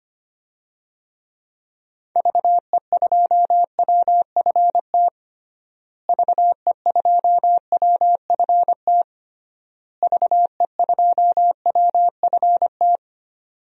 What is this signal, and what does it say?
Morse code, heard as a keyed tone, about 25 wpm, 700 Hz. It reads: VE2WFT VE2WFT VE2WFT
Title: DXCC-100 Call Signs - 6 Characters - Repeated 3x - 25wpm